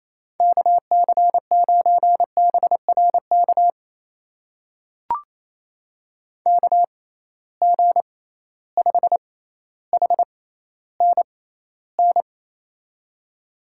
Morse code, heard as a keyed tone, about 28 wpm, spaced as for 14 wpm, 700 Hz. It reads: KC9BRK KG5HNN
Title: USA Call Signs - 6 Characters - 6x Character Spacing - 28wpm